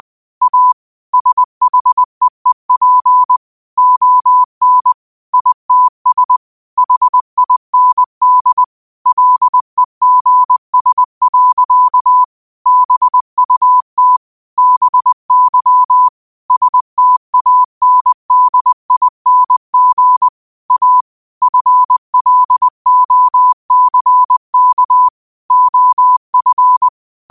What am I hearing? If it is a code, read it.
Morse code, heard as a keyed tone, about 20 wpm, 1000 Hz. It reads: A SHEEP ON ITS HIND LEGS. BUT BY STANDING A FLOCK OF
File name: qotdfiles / 20wpm